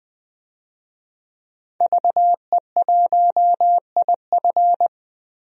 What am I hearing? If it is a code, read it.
Morse code, heard as a keyed tone, about 21 wpm, 700 Hz. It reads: VE1IF
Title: DXCC-100 Call Signs - 5 Characters - 20wpm